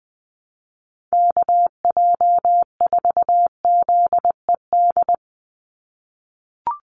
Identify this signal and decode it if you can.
Morse code, heard as a keyed tone, about 20 wpm, 700 Hz. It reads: KJ4ZED
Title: USA Call Signs 20wpm